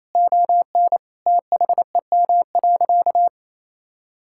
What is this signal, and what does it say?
Morse code, heard as a keyed tone, about 28 wpm, 700 Hz. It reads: ON THEM.
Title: Sentences from Top 1600 Words 28wpm